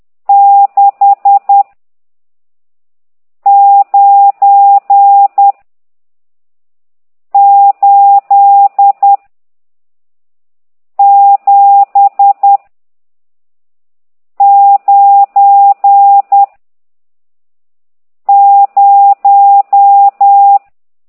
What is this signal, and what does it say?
Morse code, heard as a keyed tone, about 10 wpm, 800 Hz. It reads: 6 9 8 7 9 0
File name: ct2idw / exerc14